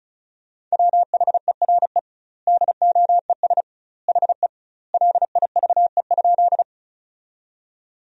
Morse code, heard as a keyed tone, about 36 wpm, 700 Hz. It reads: WHERE DOES HE LIVE?
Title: 2x Word-Spacing - Sentences from Top 200 Words - 35wpm